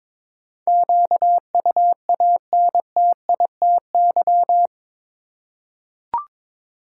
Morse code, heard as a keyed tone, about 22 wpm, 700 Hz. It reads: QUANTITY
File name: practice / 22wpm / Eight Letter Words - 22wpm